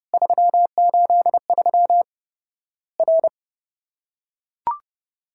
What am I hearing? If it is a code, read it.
Morse code, heard as a keyed tone, about 30 wpm, 700 Hz. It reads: 383 R
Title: Worked All Europe DX Contest - QTC Exchange - 30wpm